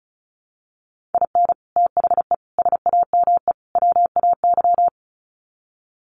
Morse code, heard as a keyed tone, about 34 wpm, 700 Hz. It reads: IN THE SAME WAY
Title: Sets of 4 Words 35wpm